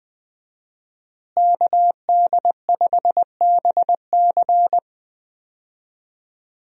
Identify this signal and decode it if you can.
Morse code, heard as a keyed tone, about 21 wpm, 700 Hz. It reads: KD5BC
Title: USA Call Signs - 5 Characters - Repeated 2x - 20wpm